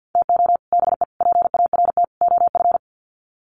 Morse code, heard as a keyed tone, about 50 wpm, 700 Hz. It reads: TO BE PART OF
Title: Sets of 4 Words 50wpm